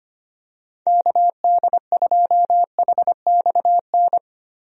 Morse code, heard as a keyed tone, about 25 wpm, 700 Hz. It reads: KD2HXN
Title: USA Call Signs - 6 Characters - Repeated 3x - 25wpm